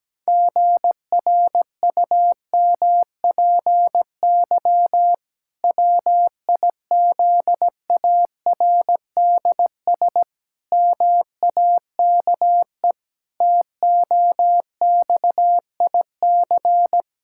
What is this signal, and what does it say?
Morse code, heard as a keyed tone, about 17 wpm, 700 Hz. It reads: GRUMPY WIZARDS MAKE TOXIC
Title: Pangram Challenge v2 - 17wpm